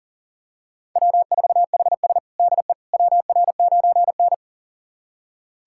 Morse code, heard as a keyed tone, about 37 wpm, 700 Hz. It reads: W4HS DE WR9N
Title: USA Callsign DE Callsign - 4 Characters - 40wpm